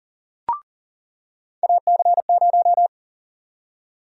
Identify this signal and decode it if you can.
Morse code, heard as a keyed tone, about 38 wpm, 700 Hz. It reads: AC0
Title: USA Call Sign Prefixes - 40wpm